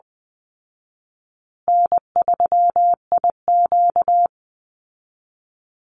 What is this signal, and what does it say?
Morse code, heard as a keyed tone, about 20 wpm, 700 Hz. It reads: N3IQ
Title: Call Signs Repeated 3x - 20wpm